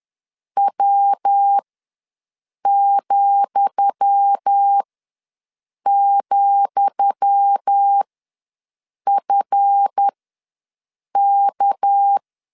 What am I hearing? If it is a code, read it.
Morse code, heard as a keyed tone, about 10 wpm, spaced as for 6 wpm, 800 Hz. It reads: W,,FK
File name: LEZIONI / 10 SIGNS / 10D COMA ws 10wpm 4314